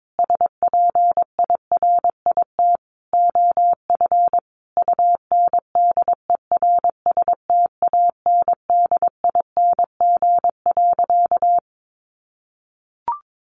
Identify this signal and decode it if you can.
Morse code, heard as a keyed tone, about 22 wpm, 700 Hz. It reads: SPIRIT OF UNDERSTANDING.
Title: Sentences from Top 1400 Words 22wpm